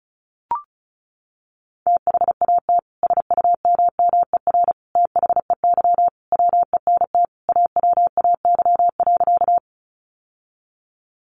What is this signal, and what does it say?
Morse code, heard as a keyed tone, about 34 wpm, 700 Hz. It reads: THAT SUMMER THEY WENT AWAY.
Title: Sentences from Top 1100 Words 35wpm